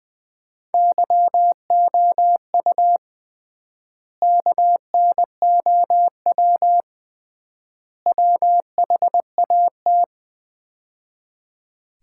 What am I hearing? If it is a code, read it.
Morse code, heard as a keyed tone, about 21 wpm, 700 Hz. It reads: YOU KNOW WHAT
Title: 3x Word-Spacing - Sets of 3 Words - 20wpm